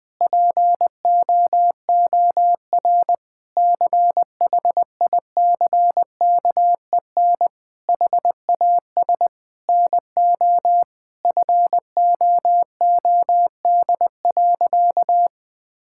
Morse code, heard as a keyed tone, about 20 wpm, 700 Hz. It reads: POOR CHICKEN HAS NO FOOD.
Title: Sentences from Top 1000 Words - Review Part 5 - 20wpm